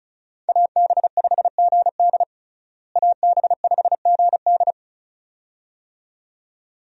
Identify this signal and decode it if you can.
Morse code, heard as a keyed tone, about 36 wpm, 700 Hz. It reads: AB5GD AB5GD
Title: USA Call Signs - 5 Characters - Repeated 2x - 35wpm